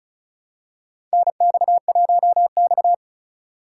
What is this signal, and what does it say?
Morse code, heard as a keyed tone, about 34 wpm, 700 Hz. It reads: NX1X
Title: USA Call Signs - 2x Character Spacing - 35wpm